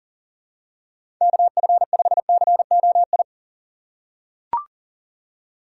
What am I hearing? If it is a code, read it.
Morse code, heard as a keyed tone, about 40 wpm, 700 Hz. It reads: KF5COI
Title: USA Call Signs - 6 Characters - 4x Character Spacing - 40wpm